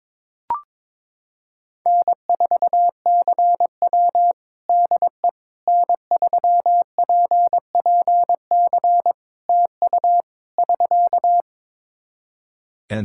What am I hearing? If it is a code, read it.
Morse code, heard as a keyed tone, about 22 wpm, 700 Hz. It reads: N4CW DE N3PPC TU <SK>
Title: Practice QSO - Exchange 4 - Ending - 22wpm